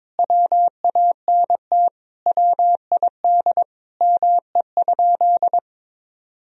Morse code, heard as a keyed tone, about 22 wpm, 700 Hz. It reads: WANT WID ME?
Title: Sentences from Top 200 Words and Abbreviations 22wpm